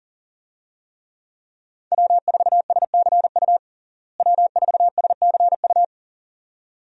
Morse code, heard as a keyed tone, about 40 wpm, 700 Hz. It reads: W4SCU W4SCU
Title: USA Call Signs - 5 Characters - Repeated 2x - 40wpm